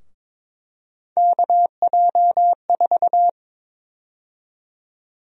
Morse code, heard as a keyed tone, about 22 wpm, 700 Hz. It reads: KJ4
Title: USA Call Sign Prefixes - 22wpm